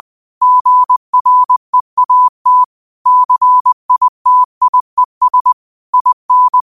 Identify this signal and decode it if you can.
Morse code, heard as a keyed tone, about 20 wpm, 1000 Hz. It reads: GREAT CITIES IN